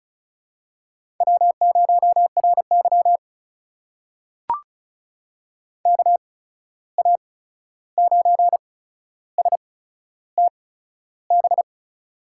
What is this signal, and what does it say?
Morse code, heard as a keyed tone, about 36 wpm, spaced as for 13 wpm, 700 Hz. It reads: W0RY KA9STB
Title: USA Call Signs - 8x Character Spacing - 35wpm